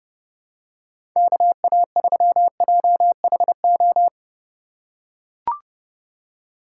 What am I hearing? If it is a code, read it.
Morse code, heard as a keyed tone, about 30 wpm, 700 Hz. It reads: KA3JHO
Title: USA Call Signs - 4x Character Spacing - 30wpm